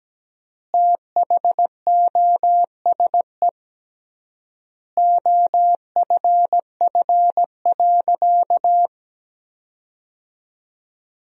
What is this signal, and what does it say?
Morse code, heard as a keyed tone, about 17 wpm, 700 Hz. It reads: THOSE OFF.
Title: 3x Word-Spacing - Sentences from Top 500 Words - 17wpm